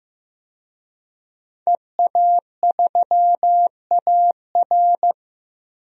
Morse code, heard as a keyed tone, about 15 wpm, 700 Hz. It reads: EA3AR
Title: DXCC-100 Call Signs - 5 Characters - 15wpm